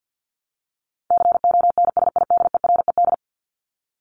Morse code, heard as a keyed tone, about 50 wpm, 700 Hz. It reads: CONSIDERED
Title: ED Suffix - Top 100 Words with Base Words 50wpm